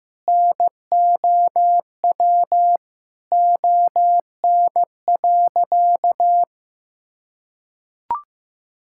Morse code, heard as a keyed tone, about 15 wpm, 700 Hz. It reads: NOW ON.
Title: Sentences from Top 1000 Words - Review Part 6 - 15wpm